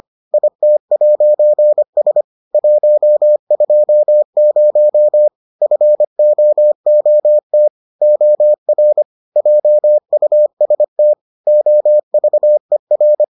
Morse code, heard as a keyed tone, about 25 wpm, 600 Hz. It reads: IT'S 120 FOOT OR JUST OVER